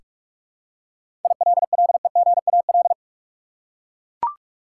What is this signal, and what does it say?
Morse code, heard as a keyed tone, about 43 wpm, 700 Hz. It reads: ILLEGAL
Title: Seven Letter Words - 45wpm